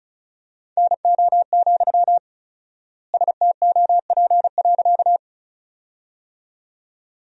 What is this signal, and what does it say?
Morse code, heard as a keyed tone, about 36 wpm, 700 Hz. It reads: NO, STOP.
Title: 4x Word-Spacing - Sentences from Top 300 Words - 35wpm